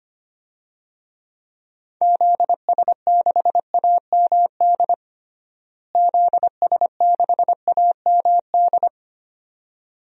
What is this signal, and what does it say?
Morse code, heard as a keyed tone, about 25 wpm, 700 Hz. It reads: ZS6AMD ZS6AMD
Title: DXCC-100 Call Signs - 6 Characters - Repeated 2x - 25wpm